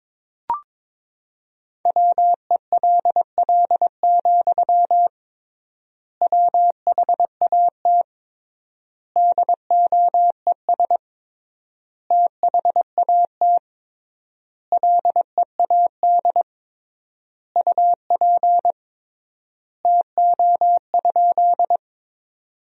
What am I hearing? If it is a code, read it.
Morse code, heard as a keyed tone, about 22 wpm, 700 Hz. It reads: WELL, WHAT DOES THAT LEAD UP TO?
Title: 3x Word-Spacing - Sentences from Top 500 Words - 22wpm